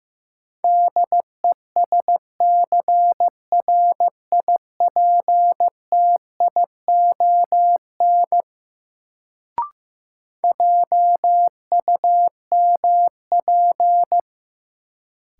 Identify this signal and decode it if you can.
Morse code, heard as a keyed tone, about 15 wpm, 700 Hz. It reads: DESCRIPTION JUMP
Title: Sentences from Top 1300 Words - New Word Drill Alt - 15wpm